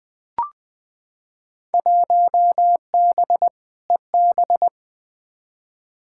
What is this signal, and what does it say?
Morse code, heard as a keyed tone, about 21 wpm, 700 Hz. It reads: 1B EB
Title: Field Day Exchanges - 20wpm